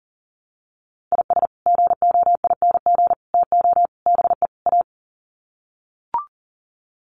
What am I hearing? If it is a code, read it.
Morse code, heard as a keyed tone, about 38 wpm, 700 Hz. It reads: IS GOING TO BE A
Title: Sets of 5 Words 40wpm